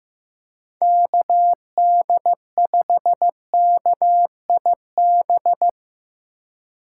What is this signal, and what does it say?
Morse code, heard as a keyed tone, about 15 wpm, 700 Hz. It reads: KD5KIB
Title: USA Call Signs - 6 Characters - 2x Character Spacing - 15wpm